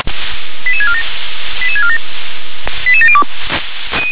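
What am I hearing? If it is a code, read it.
Five-tone selective call (ZVEI), heard as quick runs of five tones, tones of about 70 ms. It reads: 88647 88647 88702
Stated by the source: ZVEI five-tone selective call received via SDR.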